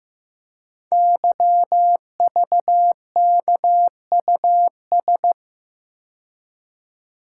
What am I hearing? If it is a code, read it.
Morse code, heard as a keyed tone, about 15 wpm, 690 Hz. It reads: YVKUS
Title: Code Groups - 5 Letters - 15wpm